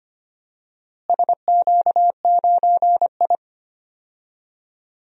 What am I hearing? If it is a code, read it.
Morse code, heard as a keyed tone, about 25 wpm, 700 Hz. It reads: SQ9I
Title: DXCC-100 Call Signs - 4 Characters - Repeated 3x - 25wpm